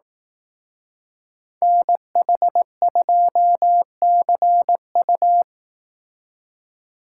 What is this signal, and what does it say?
Morse code, heard as a keyed tone, about 18 wpm, 700 Hz. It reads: NH2CU
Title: Speed Racing - USA Call Signs 15wpm